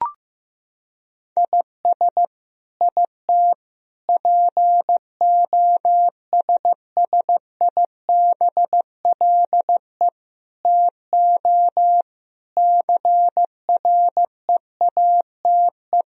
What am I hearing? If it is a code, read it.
Morse code, heard as a keyed tone, about 15 wpm, 700 Hz. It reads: IS IT POSSIBLE TO CREATE
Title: Sentences from Top 1000 Words - Review Part 3 - 15wpm